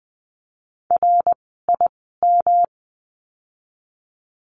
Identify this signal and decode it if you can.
Morse code, heard as a keyed tone, about 20 wpm, spaced as for 14 wpm, 700 Hz. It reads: RIM
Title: 108 Lesson 18f - Letter m - Words 20wpm